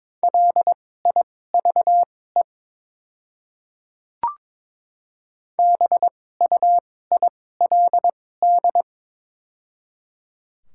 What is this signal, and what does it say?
Morse code, heard as a keyed tone, about 22 wpm, spaced as for 16 wpm, 700 Hz. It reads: LIVE BUILD
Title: 2x Character Spacing - Top 300 Words - 22wpm